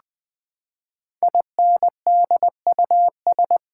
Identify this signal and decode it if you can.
Morse code, heard as a keyed tone, about 20 wpm, 700 Hz. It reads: INDUS
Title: Build A Word - 5 Letters At A Time - 10-Letter Words - 20wpm